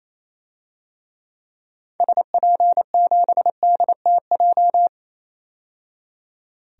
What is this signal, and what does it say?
Morse code, heard as a keyed tone, about 28 wpm, 700 Hz. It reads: SP7DTJ
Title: DXCC-100 Call Signs - 6 Characters - 28wpm